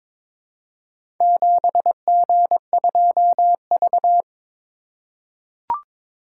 Morse code, heard as a keyed tone, about 22 wpm, 690 Hz. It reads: 7G2V
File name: practice / 22wpm / Four Letter-Number Combinations - 22wpm